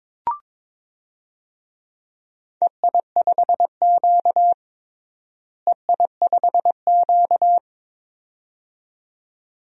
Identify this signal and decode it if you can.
Morse code, heard as a keyed tone, about 21 wpm, 700 Hz. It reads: EI5Q EI5Q
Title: DXCC-100 Call Signs - 4 Characters - Repeated 2x - 22wpm